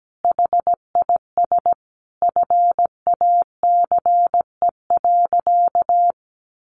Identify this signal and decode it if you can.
Morse code, heard as a keyed tone, about 17 wpm, 700 Hz. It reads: HIS FACE.
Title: Sentences from Top 300 Words 17wpm